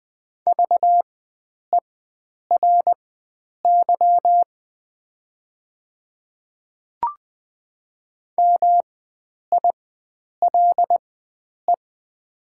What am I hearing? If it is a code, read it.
Morse code, heard as a keyed tone, about 20 wpm, spaced as for 9 wpm, 700 Hz. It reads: VERY MILE
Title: 4x Character Spacing - Top 500 Words - 20wpm